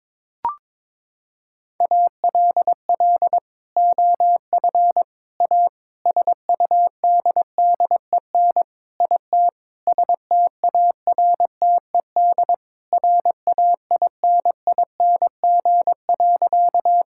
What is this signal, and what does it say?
Morse code, heard as a keyed tone, about 22 wpm, 700 Hz. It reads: ALL OF A SUDDEN IT STARTED RAINING.
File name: practice / 22wpm / Sentences with 4-Word Phrases - 22wpm